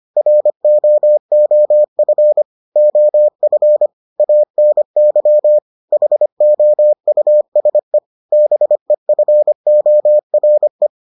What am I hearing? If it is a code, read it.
Morse code, heard as a keyed tone, about 25 wpm, 600 Hz. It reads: ROOF OF ANY HOUSE BEFORE